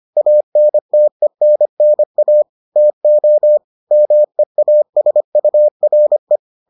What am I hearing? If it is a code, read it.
Morse code, heard as a keyed tone, about 25 wpm, 600 Hz. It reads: ANTENNA TO MEASURE